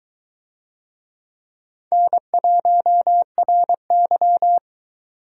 Morse code, heard as a keyed tone, about 22 wpm, 700 Hz. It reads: N1RY